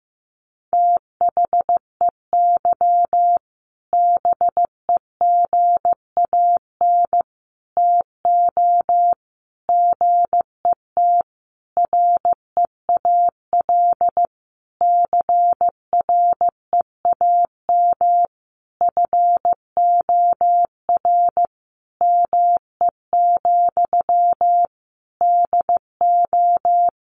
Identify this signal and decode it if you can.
Morse code, heard as a keyed tone, about 15 wpm, 700 Hz. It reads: THEY BEGAN TO GET REAL CREAM FOR ME, DO
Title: Sentences from Top 1500 Words 15wpm